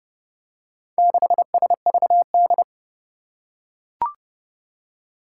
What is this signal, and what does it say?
Morse code, heard as a keyed tone, about 30 wpm, 700 Hz. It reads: 6SVD
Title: Four Challenging Letter-Number Combinations 30wpm